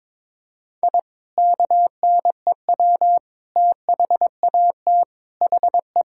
Morse code, heard as a keyed tone, about 22 wpm, 700 Hz. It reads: I KNEW THAT HE
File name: practice / 22wpm / Sentences with 2-Word Phrases - 22wpm